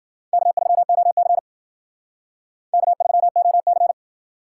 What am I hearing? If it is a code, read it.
Morse code, heard as a keyed tone, about 55 wpm, 700 Hz. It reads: K3OC K3OC